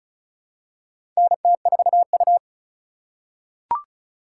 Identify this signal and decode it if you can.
Morse code, heard as a keyed tone, about 34 wpm, 700 Hz. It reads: NT4U
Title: USA Call Signs - 4 Characters - 3x Character Spacing - 35wpm